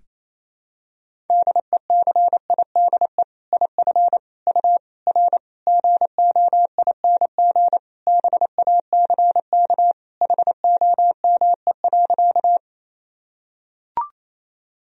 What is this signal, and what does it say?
Morse code, heard as a keyed tone, about 28 wpm, 700 Hz. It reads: DECIDE IF U R GOING BACK HOME.
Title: Sentences from Top 500 Words and Abbreviations 28wpm